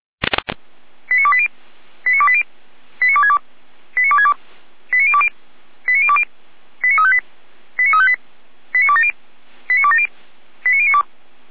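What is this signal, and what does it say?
Five-tone selective call (ZVEI), heard as quick runs of five tones, tones of about 70 ms. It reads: 89280 89280 89262 89262 89020 89020 89358 89358 89270 89270 89092
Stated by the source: ZVEI five-tone selective call 89050 received on an SDR.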